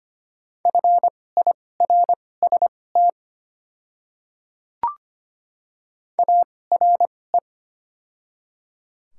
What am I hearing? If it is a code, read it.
Morse code, heard as a keyed tone, about 25 wpm, spaced as for 18 wpm, 690 Hz. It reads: FIRST ARE
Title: 2x Character Spacing - Top 200 Words - 25wpm